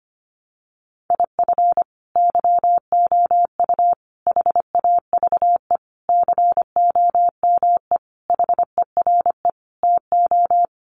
Morse code, heard as a keyed tone, about 25 wpm, 700 Hz. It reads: IF YOU HAVE COME HERE TO
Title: Sentences from Top 800 Words 25wpm